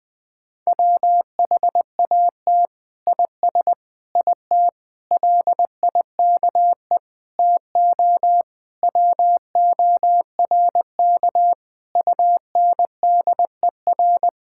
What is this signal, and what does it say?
Morse code, heard as a keyed tone, about 20 wpm, 700 Hz. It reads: WHAT IS IT LIKE TO WORK UNDER
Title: Sentences from Top 1000 Words - Review Part 6 - 20wpm